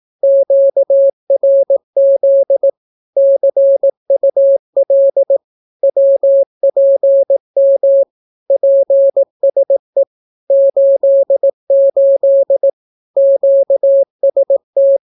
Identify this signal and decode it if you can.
Morse code, heard as a keyed tone, about 18 wpm, 550 Hz. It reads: QRZ CUL WPM PSE 88 QST
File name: naukapodstaw / skroty / skroty